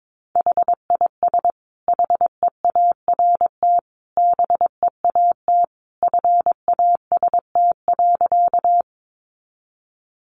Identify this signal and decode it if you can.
Morse code, heard as a keyed tone, about 22 wpm, 700 Hz. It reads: HIS HEART BEAT FAST.
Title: Sentences from Top 1300 Words 22wpm